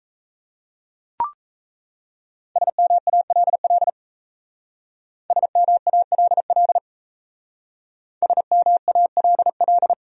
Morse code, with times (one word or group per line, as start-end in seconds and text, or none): 2.6-3.9 SMALL
5.3-6.8 SMALL
8.2-9.9 SMALL